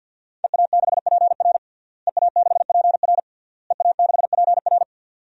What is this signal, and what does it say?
Morse code, heard as a keyed tone, about 52 wpm, 700 Hz. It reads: EA6PR EA6PR EA6PR
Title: DXCC-100 Call Signs - 5 Characters - Repeated 3x - 50wpm